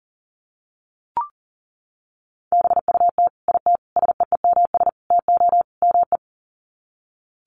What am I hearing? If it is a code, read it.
Morse code, heard as a keyed tone, about 40 wpm, 700 Hz. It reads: BUT IT SEEMS TO ME
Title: Sets of 5 Words Encore2 40wpm